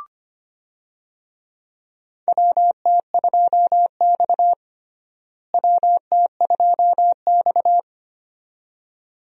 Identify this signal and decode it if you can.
Morse code, heard as a keyed tone, about 25 wpm, 700 Hz. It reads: WT2X WT2X
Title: USA Call Signs - 4 Characters - Repeated 2x - 25wpm